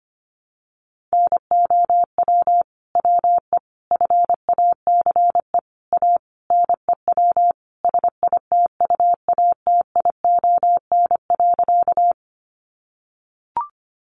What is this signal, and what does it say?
Morse code, heard as a keyed tone, about 25 wpm, 700 Hz. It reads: NOW WE FACE A NEW SITUATION.
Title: Sentences from Top 1200 Words 25wpm